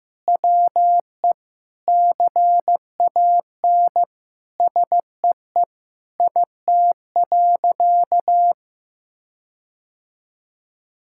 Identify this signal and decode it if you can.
Morse code, heard as a keyed tone, about 16 wpm, 700 Hz. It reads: WE CAN SEE IT.
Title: Sentences from Top 1600 Words 15wpm